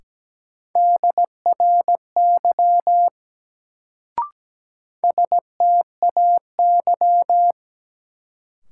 Morse code, heard as a keyed tone, about 17 wpm, 700 Hz. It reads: DRY STAY
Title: Sentences from Top 500 Words - New Word Drill - 17wpm